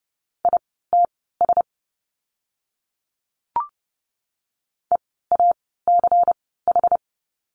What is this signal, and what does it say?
Morse code, heard as a keyed tone, about 30 wpm, spaced as for 17 wpm, 700 Hz. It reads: ITS EACH
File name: Lesson 16f - Letter c - Words 30wpm